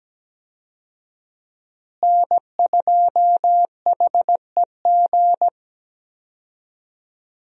Morse code, heard as a keyed tone, about 17 wpm, 700 Hz. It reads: N2HEG
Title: USA Call Signs - 5 Characters - Repeated 3x - 17wpm